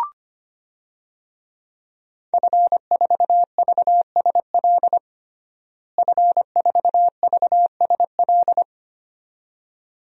Morse code, heard as a keyed tone, about 25 wpm, 700 Hz. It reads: F4VSL F4VSL
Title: DXCC-100 Call Signs - 5 Characters - Repeated 2x - 25wpm